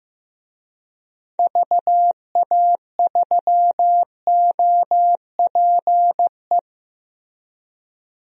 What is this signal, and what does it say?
Morse code, heard as a keyed tone, about 15 wpm, 700 Hz. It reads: VA3OPE
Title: DXCC-100 Call Signs - 6 Characters - Repeated 2x - 15wpm